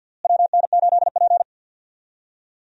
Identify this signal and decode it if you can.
Morse code, heard as a keyed tone, about 52 wpm, 700 Hz. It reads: WN8P
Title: USA Call Signs - 4 Characters - Repeated 3x - 50wpm